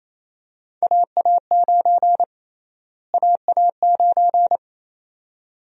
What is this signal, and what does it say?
Morse code, heard as a keyed tone, about 27 wpm, 700 Hz. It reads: AA9 AA9
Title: USA Call Sign Prefixes - Repeated 2x - 28wpm